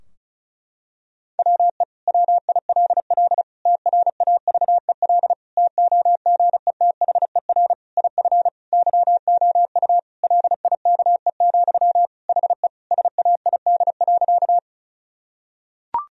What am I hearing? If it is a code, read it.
Morse code, heard as a keyed tone, about 36 wpm, 700 Hz. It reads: WE WILL TRAVEL TOGETHER IF YOU LIKE, HE SAID.
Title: Sentences from Top 1000 Words - Review Part 2 - 35wpm